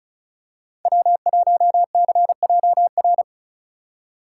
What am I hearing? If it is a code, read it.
Morse code, heard as a keyed tone, about 34 wpm, 700 Hz. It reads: W1CJR
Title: USA Call Signs - 5 Characters - 4x Character Spacing - 35wpm